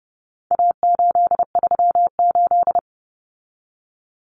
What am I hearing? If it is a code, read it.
Morse code, heard as a keyed tone, about 30 wpm, 700 Hz. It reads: A838
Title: Common Cut Numbers - 4-Digit 30wpm